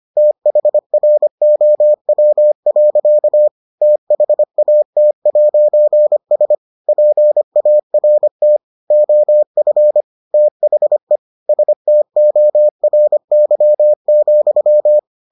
Morse code, heard as a keyed tone, about 25 wpm, 600 Hz. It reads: THROW. THAT'S PART OF THE STORY,